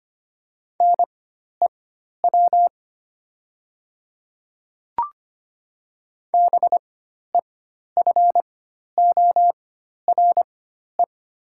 Morse code, heard as a keyed tone, about 25 wpm, spaced as for 12 wpm, 700 Hz. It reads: NEW BEFORE
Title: 4x Character Spacing - Top 200 Words - 25wpm